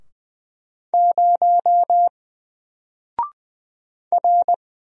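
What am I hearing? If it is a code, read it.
Morse code, heard as a keyed tone, about 20 wpm, 700 Hz. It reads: 0 R
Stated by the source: Single letter and number Morse drill at 20 wpm, from N4LCD.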